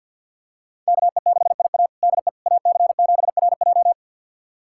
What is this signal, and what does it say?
Morse code, heard as a keyed tone, about 52 wpm, 700 Hz. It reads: KE6IA DE AC7RJ
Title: USA Callsign DE Callsign - 5 Characters - 50wpm